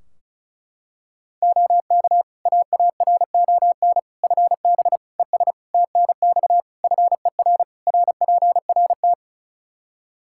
Morse code, heard as a keyed tone, about 36 wpm, 700 Hz. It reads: OK AARON FB ES TNX FER RPRT
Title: Practice QSO - Exchange 2 - Info - 35wpm